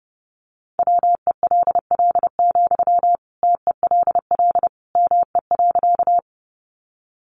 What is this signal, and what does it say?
Morse code, heard as a keyed tone, about 30 wpm, 700 Hz. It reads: WELL, TELL ME.
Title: Sentences from Top 200 Words 30wpm